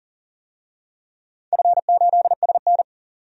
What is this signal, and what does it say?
Morse code, heard as a keyed tone, about 39 wpm, 690 Hz. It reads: F8SN
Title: DXCC-100 Call Signs - 4 Characters - 40wpm